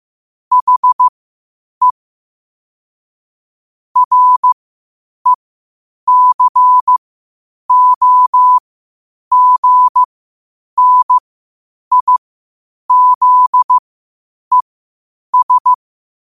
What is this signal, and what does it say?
Morse code, heard as a keyed tone, about 15 wpm, spaced as for 8 wpm, 1000 Hz. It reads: HE RECOGNIZES
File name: qotdfiles / 8wpm